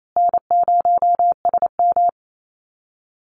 Morse code, heard as a keyed tone, about 28 wpm, 700 Hz. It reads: N0SM
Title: Call Signs Repeated 2x - 28wpm